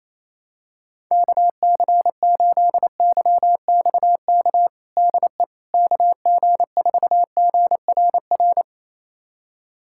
Morse code, heard as a keyed tone, about 28 wpm, 700 Hz. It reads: KC8YXK DE KG4GRR
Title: USA Callsign DE Callsign - 6 Characters - 28wpm